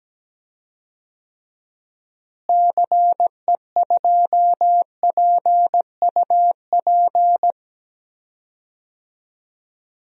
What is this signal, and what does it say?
Morse code, heard as a keyed tone, about 17 wpm, 700 Hz. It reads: CE2PUP
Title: DXCC-100 Call Signs - 6 Characters - Repeated 3x - 17wpm